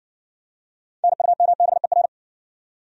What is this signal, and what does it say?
Morse code, heard as a keyed tone, about 60 wpm, 700 Hz. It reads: NUMBER